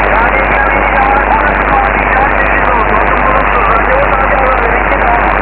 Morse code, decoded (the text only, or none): none